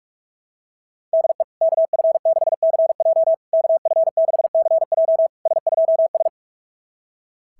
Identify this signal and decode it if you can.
Morse code, heard as a keyed tone, about 44 wpm, 690 Hz. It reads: DE KF6CJ KF6CJ S2S
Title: SOTA QSO Practice 45wpm